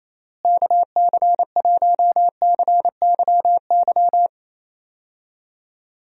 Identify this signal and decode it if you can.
Morse code, heard as a keyed tone, about 28 wpm, 700 Hz. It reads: KC1CYY
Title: USA Call Signs - 6 Characters - Repeated 2x - 28wpm